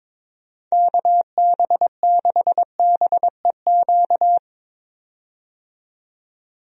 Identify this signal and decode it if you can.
Morse code, heard as a keyed tone, about 22 wpm, 700 Hz. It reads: KB6BEQ
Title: USA Call Signs - 6 Characters - 22wpm